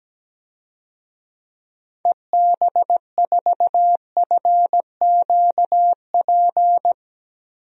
Morse code, heard as a keyed tone, about 17 wpm, 700 Hz. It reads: EB4FQP
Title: DXCC-100 Call Signs - 6 Characters - 17wpm